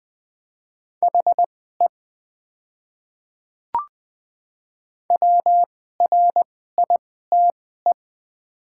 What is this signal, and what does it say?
Morse code, heard as a keyed tone, about 20 wpm, spaced as for 14 wpm, 700 Hz. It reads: HE WRITE
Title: 2x Character Spacing - Top 200 Words - 20wpm